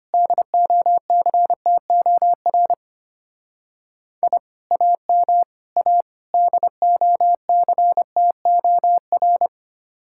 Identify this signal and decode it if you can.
Morse code, heard as a keyed tone, about 29 wpm, 700 Hz. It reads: DOCTOR I AM A DOCTOR
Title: Speed-Racing Common QSO Phrases - 25wpm